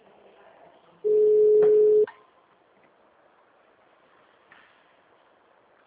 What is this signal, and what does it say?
Ringback tone (Europe, 425 Hz).
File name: out- -2024- - - .24